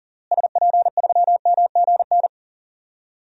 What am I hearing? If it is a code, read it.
Morse code, heard as a keyed tone, about 40 wpm, 700 Hz. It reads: SP3MGN
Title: DXCC-100 Call Signs - 6 Characters - Repeated 3x - 40wpm